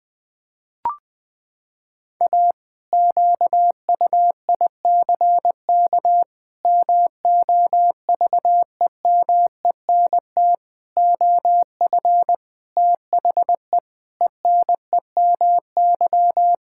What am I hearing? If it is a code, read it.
Morse code, heard as a keyed tone, about 20 wpm, 700 Hz. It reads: A QUICK MOVEMENT OF THE ENEMY
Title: Pangram Challenge v2 - 20wpm